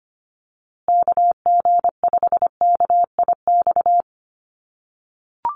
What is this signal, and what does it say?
Morse code, heard as a keyed tone, about 25 wpm, 700 Hz. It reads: KG5KIX
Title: USA Call Signs Repeated 2x 25wpm